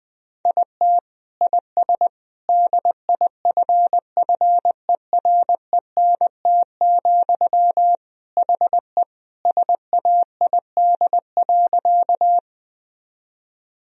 Morse code, heard as a keyed tone, about 20 wpm, 700 Hz. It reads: IT IS DIFFERENT, HE SAID.
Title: Sentences from Top 1000 Words - Review Part 3 - 20wpm